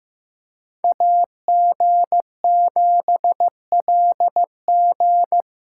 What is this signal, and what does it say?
Morse code, heard as a keyed tone, about 18 wpm, 700 Hz. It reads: AG7LG
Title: Speed Racing - USA 2x2 Call Signs 15wpm